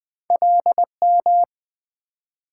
Morse code, heard as a keyed tone, about 26 wpm, 700 Hz. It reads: LM